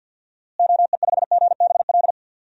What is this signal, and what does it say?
Morse code, heard as a keyed tone, about 52 wpm, 700 Hz. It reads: OE5GBL
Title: DXCC-100 Call Signs - 6 Characters - Repeated 2x - 50wpm